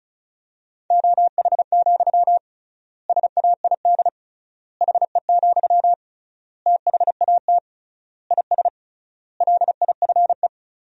Morse code, heard as a keyed tone, about 36 wpm, 700 Hz. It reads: OH, SAID HE, THAT IS LIFE
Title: 3x Word-Spacing - Sentences from Top 500 Words - 35wpm